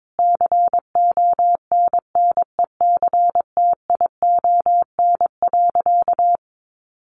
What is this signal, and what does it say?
Morse code, heard as a keyed tone, about 22 wpm, 700 Hz. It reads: CONNECTION.
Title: Sentences from Top 1300 Words 22wpm